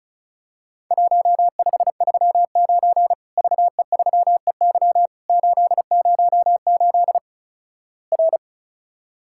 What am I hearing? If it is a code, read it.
Morse code, heard as a keyed tone, about 36 wpm, 700 Hz. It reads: 1539 VE3EY 808 R
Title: Worked All Europe DX Contest - QTC Exchange - 35wpm